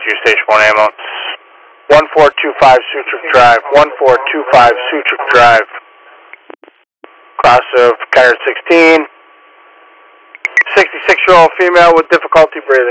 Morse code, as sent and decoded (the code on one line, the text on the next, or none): none